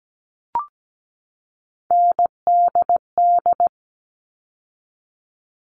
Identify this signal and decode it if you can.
Morse code, heard as a keyed tone, about 17 wpm, 700 Hz. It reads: NDD
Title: All Cut Numbers - 3-Digit 17wpm